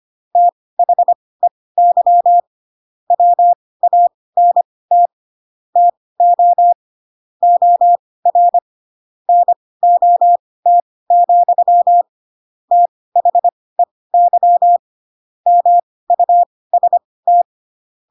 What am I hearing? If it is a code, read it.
Morse code, heard as a keyed tone, about 25 wpm, spaced as for 18 wpm, 700 Hz. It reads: THEY WANT TO OR NOT, THEY MUST
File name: Pinocchio / Pinocchio ch4 12 25 ik1hge